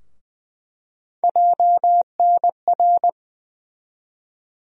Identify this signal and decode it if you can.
Morse code, heard as a keyed tone, about 20 wpm, 700 Hz. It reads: JNR
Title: USA Call Sign Postfixes - 3 Letters 20wpm